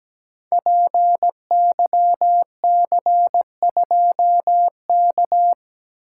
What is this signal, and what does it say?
Morse code, heard as a keyed tone, about 17 wpm, 700 Hz. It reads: PYC2K
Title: Code Groups - 5 Letter:Numbers - 17wpm